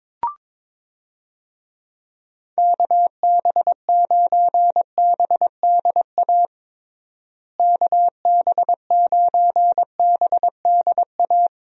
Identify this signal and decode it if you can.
Morse code, heard as a keyed tone, about 22 wpm, 700 Hz. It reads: KB9BDA KB9BDA
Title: USA Call Signs - 6 Characters - Repeated 2x - 22wpm